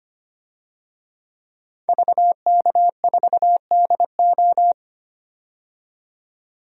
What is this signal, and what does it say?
Morse code, heard as a keyed tone, about 25 wpm, 700 Hz. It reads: VK4DO
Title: DXCC-100 Call Signs - 5 Characters - 25wpm